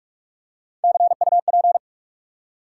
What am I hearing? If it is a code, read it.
Morse code, heard as a keyed tone, about 44 wpm, 700 Hz. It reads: CUP
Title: Three Letter Words - 45wpm